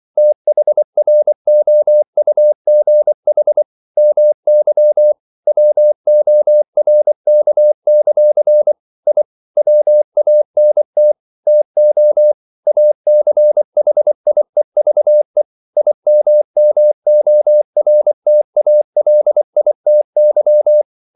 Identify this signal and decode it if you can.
Morse code, heard as a keyed tone, about 24 wpm, 600 Hz. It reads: THROUGH MY WORK; I WANT TO ACHIEVE IMMORTALITY
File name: CW / Practice / EnglishQuotations0424